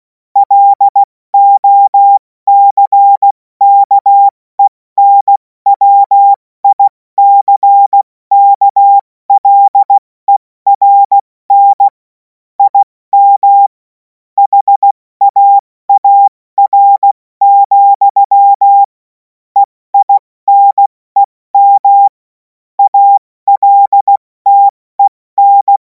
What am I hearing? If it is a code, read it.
Morse code, heard as a keyed tone, about 16 wpm, spaced as for 14 wpm, 800 Hz. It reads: LOCKENWICKLERN IM HAAR, EINEM ALTEN
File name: Morsekurs / Witze / Witz 23. Juni 2026 70 F80